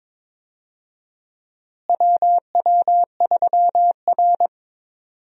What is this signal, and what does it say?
Morse code, heard as a keyed tone, about 22 wpm, 700 Hz. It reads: WW3R
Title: USA Call Signs - 4 Characters - Repeated 2x - 22wpm